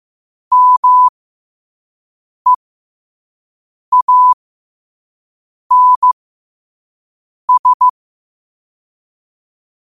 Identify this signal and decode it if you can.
Morse code, heard as a keyed tone, about 15 wpm, spaced as for 5 wpm, 1000 Hz. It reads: MEANS